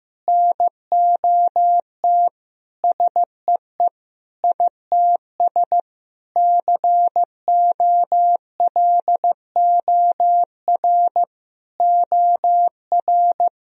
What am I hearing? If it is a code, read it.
Morse code, heard as a keyed tone, about 15 wpm, 700 Hz. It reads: NOT SEE ITS COLOR OR
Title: Sentences from Top 1000 Words - Review Part 3 - 15wpm